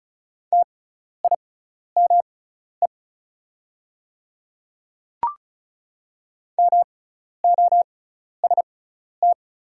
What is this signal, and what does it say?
Morse code, heard as a keyed tone, about 35 wpm, spaced as for 12 wpm, 690 Hz. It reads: TIME MOST
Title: 6x Character Spacing - Top 100 Words - 35wpm